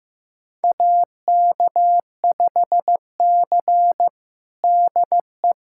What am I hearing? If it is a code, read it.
Morse code, heard as a keyed tone, about 15 wpm, 700 Hz. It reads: AK5C DE
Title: USA Callsign DE Callsign - 4 Characters - 15wpm